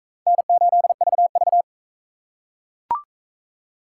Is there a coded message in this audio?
Yes: Morse code from 0.3 to 1.6 s, heard as a keyed tone, about 37 wpm, 700 Hz. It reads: N8VV